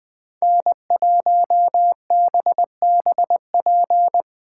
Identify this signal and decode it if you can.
Morse code, heard as a keyed tone, about 24 wpm, 700 Hz. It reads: N1BBP
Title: Speed Racing - USA 1x3 Call Signs 20wpm